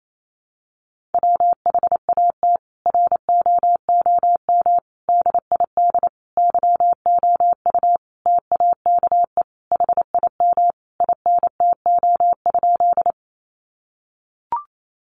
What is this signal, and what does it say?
Morse code, heard as a keyed tone, about 28 wpm, 700 Hz. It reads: WHAT ROOM DID YOU TAKE HIM INTO?
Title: Sentences from Top 600 Words 28wpm